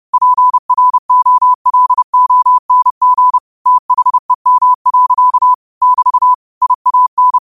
Morse code, heard as a keyed tone, about 30 wpm, 1000 Hz. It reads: PROLONG THEM. <BT> IAN
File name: qotdfiles / 30wpm